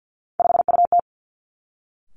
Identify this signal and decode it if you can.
Morse code, heard as a keyed tone, about 50 wpm, 700 Hz. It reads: 5UT
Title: Three Challenging Letter-Number Combinations 50wpm